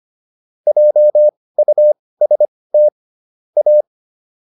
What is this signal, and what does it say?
Morse code, heard as a keyed tone, about 25 wpm, spaced as for 18 wpm, 600 Hz. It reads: JUST A
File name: WD104 18